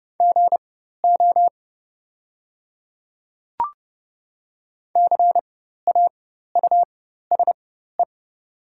Morse code, heard as a keyed tone, about 30 wpm, spaced as for 14 wpm, 700 Hz. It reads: GO CAUSE
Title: 4x Character Spacing - Top 200 Words - 30wpm